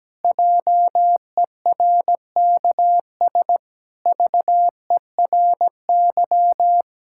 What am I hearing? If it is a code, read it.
Morse code, heard as a keyed tone, about 17 wpm, 700 Hz. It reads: JERKS VERY
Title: Pangram Challenge v2 - 17wpm